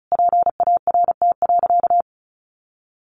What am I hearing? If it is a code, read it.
Morse code, heard as a keyed tone, about 34 wpm, 700 Hz. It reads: PART.
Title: Sentences from Top 1200 Words 35wpm